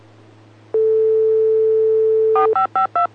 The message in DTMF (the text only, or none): 4552